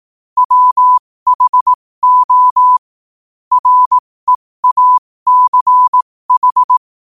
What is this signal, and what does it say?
Morse code, heard as a keyed tone, about 18 wpm, spaced as for 15 wpm, 1000 Hz. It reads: WHO REACH
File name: qotdfiles / 15wpm